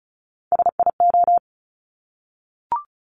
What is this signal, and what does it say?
Morse code, heard as a keyed tone, about 36 wpm, 700 Hz. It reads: SIO
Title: Top Three Letter Combinations in Words 35wpm